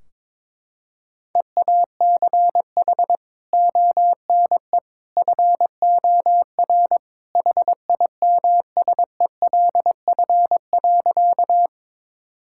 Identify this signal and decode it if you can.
Morse code, heard as a keyed tone, about 21 wpm, 700 Hz. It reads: EACH ONE FOR HIMSELF.
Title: Sentences from Top 1600 Words 22wpm